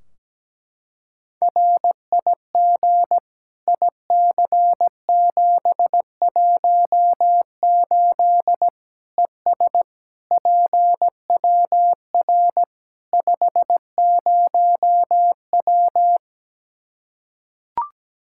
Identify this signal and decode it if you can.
Morse code, heard as a keyed tone, about 17 wpm, 700 Hz. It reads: RIG IC718 ES PWR 50W
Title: Practice QSO - Exchange 2 - Info - 17wpm